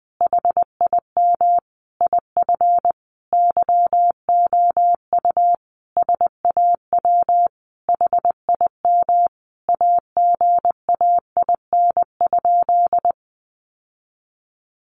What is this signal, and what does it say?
Morse code, heard as a keyed tone, about 20 wpm, 700 Hz. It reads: HIM IF YOU SAW HIM AGAIN?
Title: Sentences from Top 1400 Words 20wpm